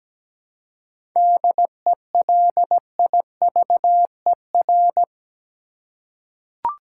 Morse code, heard as a keyed tone, about 17 wpm, 700 Hz. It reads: DELIVER
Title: Seven Letter Words - 17wpm